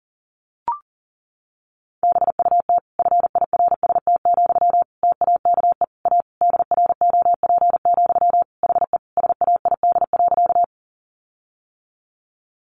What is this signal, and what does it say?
Morse code, heard as a keyed tone, about 40 wpm, 700 Hz. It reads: BUT FIRST, TAKE A DROP, HE SAID.
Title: Sentences from Top 1400 Words 40wpm